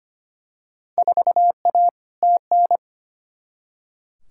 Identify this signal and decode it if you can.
Morse code, heard as a keyed tone, about 25 wpm, 700 Hz. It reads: 4A TN
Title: Field Day Exchanges - 25wpm